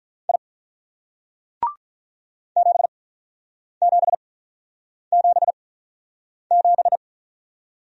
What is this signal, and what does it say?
Morse code, heard as a keyed tone, about 46 wpm, 700 Hz. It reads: I 7 7 7 7